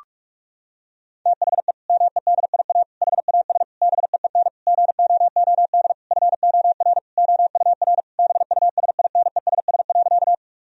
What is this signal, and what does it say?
Morse code, heard as a keyed tone, about 41 wpm, 700 Hz. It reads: THE MEDIA HAS BEEN GOOD FOR OUR BUSINESS.
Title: Sentences from Top 1000 Words - Review Part 4 - 45wpm